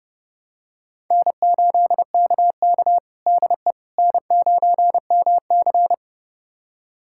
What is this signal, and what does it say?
Morse code, heard as a keyed tone, about 30 wpm, 700 Hz. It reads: N8KK DE N9MC
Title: USA Callsign DE Callsign - 4 Characters - 30wpm